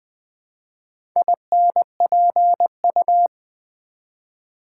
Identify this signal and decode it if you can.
Morse code, heard as a keyed tone, about 20 wpm, 700 Hz. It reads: INPU
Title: Build A Word - 1 Letter At A Time - 5-Letter Words - 20wpm